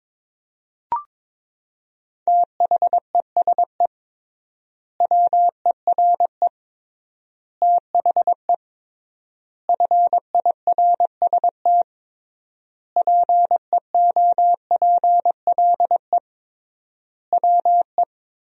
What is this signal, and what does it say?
Morse code, heard as a keyed tone, about 22 wpm, 700 Hz. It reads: THESE WERE THE FIRST PEOPLE WE
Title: 3x Word-Spacing - Sentences from Top 300 Words - 22wpm